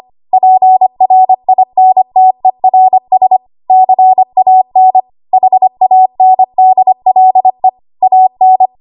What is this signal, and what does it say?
Morse code, heard as a keyed tone, about 25 wpm, 750 Hz. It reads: PRINTERS CAN HANDLE AN
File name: Archive / 25 WPM / 25WPM